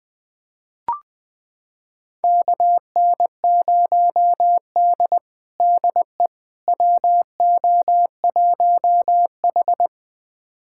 Morse code, heard as a keyed tone, about 20 wpm, 700 Hz. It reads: KN0D DE WO1H
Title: USA Callsign DE Callsign - 4 Characters - 20wpm